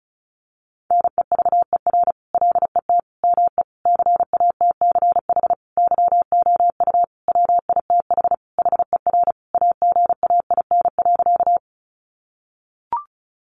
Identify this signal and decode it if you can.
Morse code, heard as a keyed tone, about 34 wpm, 700 Hz. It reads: NEVER LET ME CATCH YOU WITH HER AGAIN.
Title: Sentences from Top 1500 Words 35wpm